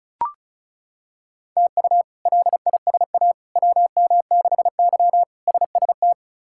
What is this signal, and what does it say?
Morse code, heard as a keyed tone, about 36 wpm, 700 Hz. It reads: TU LISA WM6Y SST
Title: K1USN SST QSO Practice - 35wpm